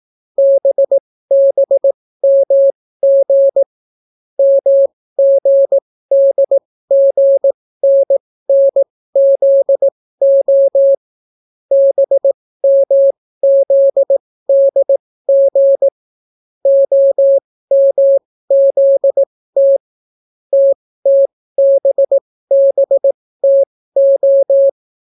Morse code, heard as a keyed tone, about 18 wpm, spaced as for 15 wpm, 550 Hz. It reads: BBMG MGDGNNZO BMZDG OMZT TTBBTO